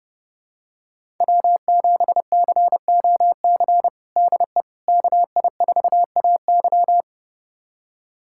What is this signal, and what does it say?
Morse code, heard as a keyed tone, about 31 wpm, 700 Hz. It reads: W7COC DE KI4AY
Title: USA Callsign DE Callsign - 5 Characters - 30wpm